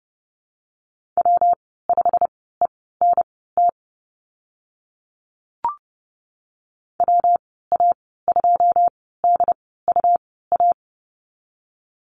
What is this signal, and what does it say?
Morse code, heard as a keyed tone, about 30 wpm, spaced as for 17 wpm, 700 Hz. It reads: W5ENT WA2DUA
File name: Lesson 19g - Letter w - Call Signs 30wpm